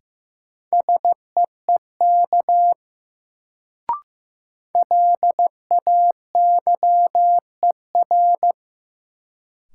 Morse code, heard as a keyed tone, about 15 wpm, 700 Hz. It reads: SEEK LAYER
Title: Sentences from Top 1600 Words - New Word Drill - 15wpm